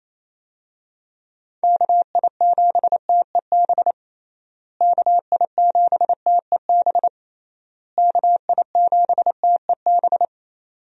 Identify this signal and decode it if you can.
Morse code, heard as a keyed tone, about 28 wpm, 700 Hz. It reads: KI7TEB KI7TEB KI7TEB
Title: USA Call Signs - 6 Characters - Repeated 3x - 28wpm